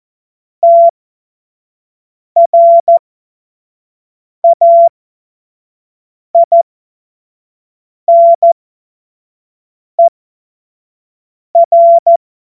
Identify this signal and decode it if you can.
Morse code, heard as a keyed tone, about 14 wpm, spaced as for 5 wpm, 680 Hz. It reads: TRAINER
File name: Morsekurs / Witze / Witz 09. Februar 2021, 25 F70